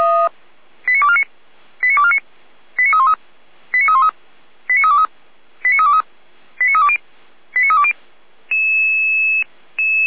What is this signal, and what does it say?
Five-tone selective call (ZVEI), heard as quick runs of five tones, tones of about 70 ms. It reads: 89249 89249 89313 89313 89323 89323 89320 89320
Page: /455/SDR/ZVEI/